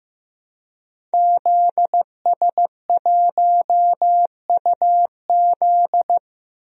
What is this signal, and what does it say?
Morse code, heard as a keyed tone, about 15 wpm, 700 Hz. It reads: ZS1UZ
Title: DXCC-100 Call Signs - 5 Characters - Repeated 2x - 15wpm